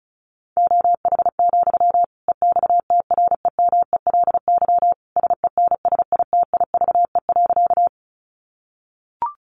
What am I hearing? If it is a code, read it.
Morse code, heard as a keyed tone, about 34 wpm, 700 Hz. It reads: OH, EXTREMELY SENSITIVE.
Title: Sentences from Top 1400 Words 35wpm